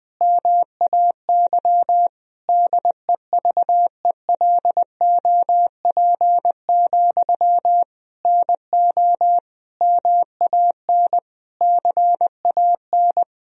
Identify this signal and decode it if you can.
Morse code, heard as a keyed tone, about 20 wpm, 700 Hz. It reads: MAY DEVELOP, NO MAN CAN
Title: Sentences from Top 1000 Words - Review Part 2 - 20wpm